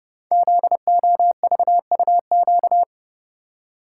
Morse code, heard as a keyed tone, about 28 wpm, 700 Hz. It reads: ZOVUQ
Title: Code Groups - 5 Letters - 30wpm